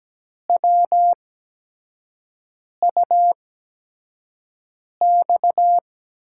Morse code, heard as a keyed tone, about 17 wpm, spaced as for 5 wpm, 700 Hz. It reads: WUX
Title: USA Call Signs - 6 Characters - 8x Character Spacing - 17wpm